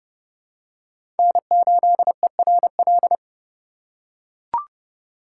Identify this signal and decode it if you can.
Morse code, heard as a keyed tone, about 30 wpm, 700 Hz. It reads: N8ERL
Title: USA Call Signs - 5 Characters - 2x Character Spacing - 30wpm